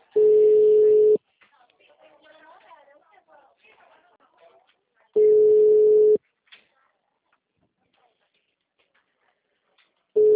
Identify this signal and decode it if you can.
Ringback tone (Europe, 425 Hz), with speech.